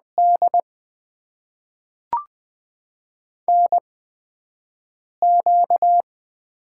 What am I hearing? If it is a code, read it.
Morse code, heard as a keyed tone, about 20 wpm, spaced as for 5 wpm, 700 Hz. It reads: D NQ
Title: USA Call Signs - 4 Characters - 8x Character Spacing - 20wpm